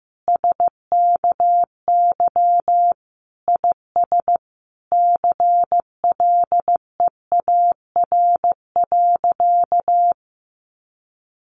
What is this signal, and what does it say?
Morse code, heard as a keyed tone, about 15 wpm, 700 Hz. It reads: SKY IS CLEAR.
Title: Sentences from Top 1200 Words 15wpm